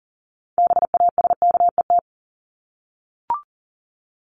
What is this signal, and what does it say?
Morse code, heard as a keyed tone, about 37 wpm, 700 Hz. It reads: BASKET
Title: Top 1751-2000 Words 40wpm